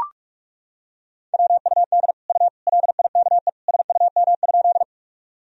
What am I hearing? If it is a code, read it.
Morse code, heard as a keyed tone, about 42 wpm, 700 Hz. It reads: WUD U LIKE SUM?
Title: Sentences from Top 100 Words and Abbreviations 45wpm